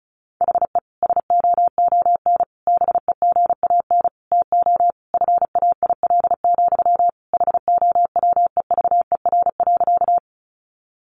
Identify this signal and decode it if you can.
Morse code, heard as a keyed tone, about 34 wpm, 700 Hz. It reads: HE SOON BEGAN TO FAIL, HOWEVER.
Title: Sentences from Top 1100 Words 35wpm